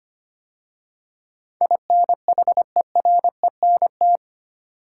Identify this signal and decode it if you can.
Morse code, heard as a keyed tone, about 25 wpm, 700 Hz. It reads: INHERENT
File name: 8 Letter Words - First 3 Letters Given - 25wpm